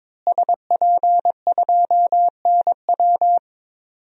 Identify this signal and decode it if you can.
Morse code, heard as a keyed tone, about 22 wpm, 700 Hz. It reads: SP2NW
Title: DXCC-100 Call Signs - 5 Characters - Repeated 2x - 22wpm